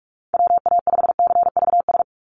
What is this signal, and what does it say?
Morse code, heard as a keyed tone, about 44 wpm, 700 Hz. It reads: WA5CVS
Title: USA Call Signs 45wpm